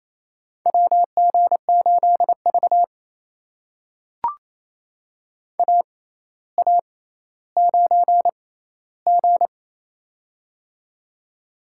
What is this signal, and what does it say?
Morse code, heard as a keyed tone, about 27 wpm, spaced as for 14 wpm, 700 Hz. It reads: WG8V AA9G
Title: USA Call Signs - 4 Characters - 6x Character Spacing - 28wpm